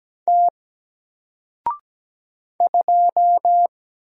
Morse code, heard as a keyed tone, about 17 wpm, 700 Hz. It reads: T 2
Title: Single Letter-Number 17wpm